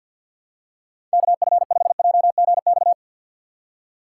Morse code, heard as a keyed tone, about 48 wpm, 700 Hz. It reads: KF5JGX
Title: USA Call Signs - 6 Characters - 3x Character Spacing - 50wpm